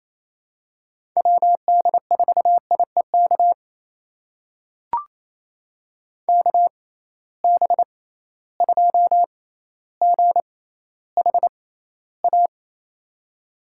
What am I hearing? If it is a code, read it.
Morse code, heard as a keyed tone, about 28 wpm, spaced as for 14 wpm, 700 Hz. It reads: WD4IEK KB2GHA
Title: USA Call Signs - 6 Characters - 6x Character Spacing - 28wpm